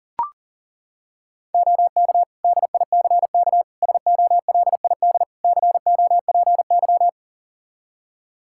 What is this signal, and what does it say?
Morse code, heard as a keyed tone, about 40 wpm, 700 Hz. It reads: OK DICK SOLID COPY
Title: Practice QSO - Exchange 3 - More Info - 40wpm